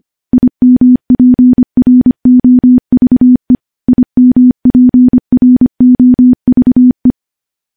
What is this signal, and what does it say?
Morse code, heard as a keyed tone, about 25 wpm, 250 Hz. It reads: IMPROVE IMPROVE
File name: morse / morse-training-25WPM